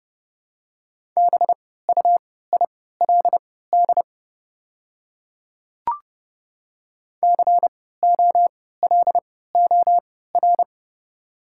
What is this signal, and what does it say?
Morse code, heard as a keyed tone, about 30 wpm, spaced as for 17 wpm, 690 Hz. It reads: BUILD COLOR
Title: 3x Character Spacing - Top 300 Words - 30wpm